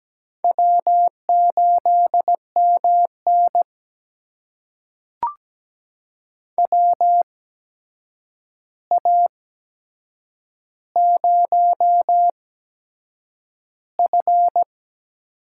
Morse code, heard as a keyed tone, about 17 wpm, spaced as for 7 wpm, 700 Hz. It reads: W8MN WA0F
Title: USA Call Signs - 4 Characters - 8x Character Spacing - 17wpm